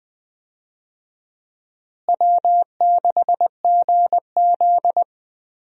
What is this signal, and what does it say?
Morse code, heard as a keyed tone, about 20 wpm, 700 Hz. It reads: W6GZ
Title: USA Call Signs - 4 Characters - Repeated 2x - 20wpm